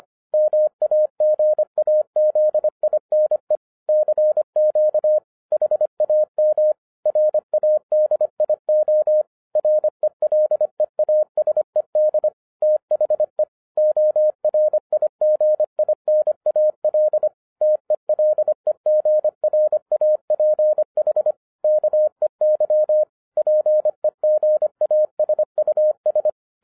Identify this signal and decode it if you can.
Morse code, heard as a keyed tone, about 25 wpm, 610 Hz. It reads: MAGAZINE CQ HAM RADIO RELEASED THE ORIGINAL TELEGRAPH KEY PEGASUS